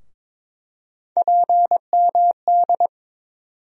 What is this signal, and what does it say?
Morse code, heard as a keyed tone, about 22 wpm, 700 Hz. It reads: PMD
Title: USA Call Sign Postfixes - 3 Letters 22wpm